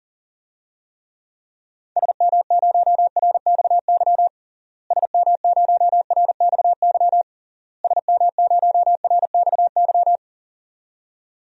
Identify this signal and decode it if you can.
Morse code, heard as a keyed tone, about 40 wpm, 700 Hz. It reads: SM0RXY SM0RXY SM0RXY
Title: DXCC-100 Call Signs - 6 Characters - Repeated 3x - 40wpm